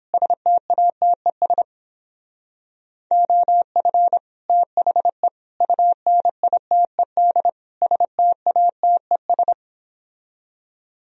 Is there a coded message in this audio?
Yes: Morse code from 0.1 to 9.5 s, heard as a keyed tone, about 29 wpm, 700 Hz. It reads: STATES OF THE UNITED STATES